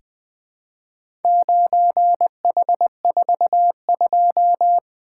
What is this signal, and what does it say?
Morse code, heard as a keyed tone, about 20 wpm, 700 Hz. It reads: 9H42
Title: Four Letter-Number Combinations - 20wpm